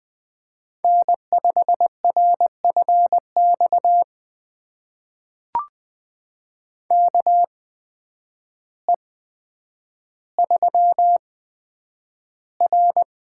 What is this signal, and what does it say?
Morse code, heard as a keyed tone, about 20 wpm, spaced as for 9 wpm, 700 Hz. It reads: N5RFX KE3R
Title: USA Call Signs - 8x Character Spacing - 20wpm